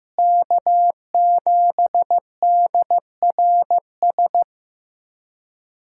Morse code, heard as a keyed tone, about 15 wpm, 700 Hz. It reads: K7DRS
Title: USA Callsign DE Callsign - 5 Characters - 15wpm